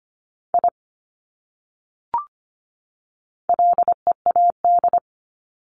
Morse code, heard as a keyed tone, about 25 wpm, 700 Hz. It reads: I LEAD
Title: Top 500 Words 25wpm